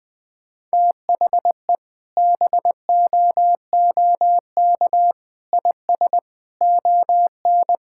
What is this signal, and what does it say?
Morse code, heard as a keyed tone, about 20 wpm, 700 Hz. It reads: THE BOOK IS ON
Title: Sentences with 3-Word Phrases - 20wpm